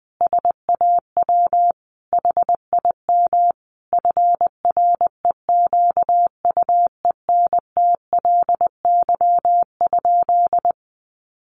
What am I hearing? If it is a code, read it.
Morse code, heard as a keyed tone, about 20 wpm, 700 Hz. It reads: SAW HIM FREQUENTLY?
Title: Sentences from Top 1300 Words 20wpm